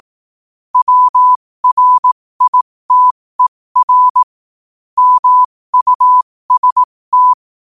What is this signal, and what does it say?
Morse code, heard as a keyed tone, about 18 wpm, spaced as for 15 wpm, 1000 Hz. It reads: WRITER MUST
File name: qotdfiles / 15wpm